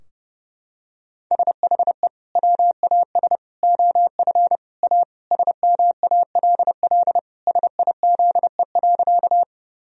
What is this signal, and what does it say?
Morse code, heard as a keyed tone, about 30 wpm, 700 Hz. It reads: SHE WAS OF A SMALL SIZE.
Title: Sentences from Top 1000 Words - Review Part 2 - 30wpm